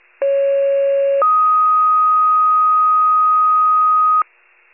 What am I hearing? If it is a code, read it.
Two-tone page, heard as a one-second tone, then a three-second tone, about 550 then 1250 Hz. It is heard from 0.2 to 4.2 s.